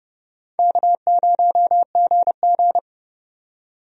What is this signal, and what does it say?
Morse code, heard as a keyed tone, about 31 wpm, 700 Hz. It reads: K0GG
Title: USA Call Signs - 4 Characters - 30wpm